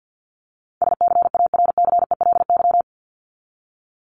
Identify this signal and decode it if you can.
Morse code, heard as a keyed tone, about 50 wpm, 700 Hz. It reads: SCARCELY
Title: LY Suffix - Top 100 Words with Base Words 50wpm